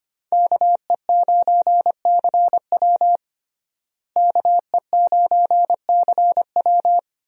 Morse code, heard as a keyed tone, about 25 wpm, 700 Hz. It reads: KE9CW KE9CW
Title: USA Call Signs - 5 Characters - Repeated 3x - 25wpm